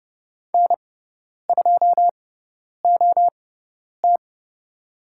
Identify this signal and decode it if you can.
Morse code, heard as a keyed tone, about 30 wpm, spaced as for 10 wpm, 700 Hz. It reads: N2OT
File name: Lesson 16g - Letter c - Call Signs 30wpm